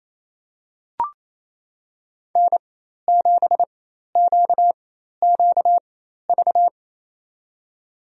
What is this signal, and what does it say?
Morse code, heard as a keyed tone, about 28 wpm, spaced as for 13 wpm, 690 Hz. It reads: N7QQV
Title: USA Call Signs - 5 Characters - 4x Character Spacing - 28wpm